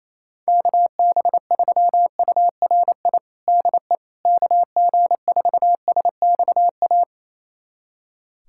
Morse code, heard as a keyed tone, about 28 wpm, 700 Hz. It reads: KB3URI DE KG4SXA
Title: USA Callsign DE Callsign 28wpm